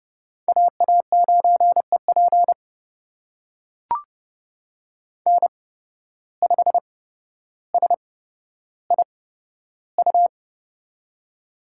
Morse code, heard as a keyed tone, about 28 wpm, spaced as for 13 wpm, 700 Hz. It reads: AA9EP N5SIU
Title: USA Call Signs - 5 Characters - 8x Character Spacing - 30wpm